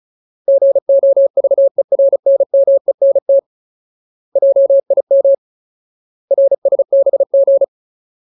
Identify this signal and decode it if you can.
Morse code, heard as a keyed tone, about 34 wpm, 550 Hz. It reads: GOVERNMENT JIM RSBG
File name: Ultimate Word List / 35WPM Ultimate